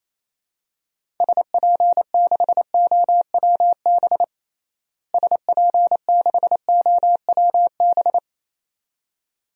Morse code, heard as a keyed tone, about 28 wpm, 700 Hz. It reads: SP6OWB SP6OWB
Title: DXCC-100 Call Signs - 6 Characters - Repeated 2x - 28wpm